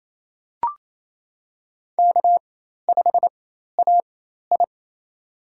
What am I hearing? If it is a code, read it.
Morse code, heard as a keyed tone, about 27 wpm, spaced as for 13 wpm, 700 Hz. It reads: K5AI
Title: USA Call Signs - 4 Characters - 4x Character Spacing - 28wpm